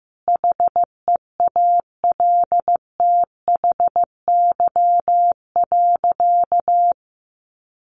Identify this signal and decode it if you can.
Morse code, heard as a keyed tone, about 15 wpm, 700 Hz. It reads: HEALTHY.
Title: Easy Phrases 15wpm